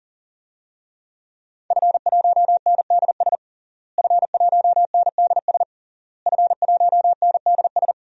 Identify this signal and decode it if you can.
Morse code, heard as a keyed tone, about 40 wpm, 700 Hz. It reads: F1NDS F1NDS F1NDS
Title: DXCC-100 Call Signs - 5 Characters - Repeated 3x - 40wpm